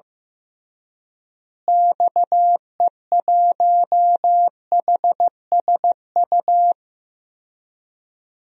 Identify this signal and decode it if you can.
Morse code, heard as a keyed tone, about 16 wpm, 700 Hz. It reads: XE1HSU
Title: DXCC-100 Call Signs - 6 Characters - Repeated 3x - 15wpm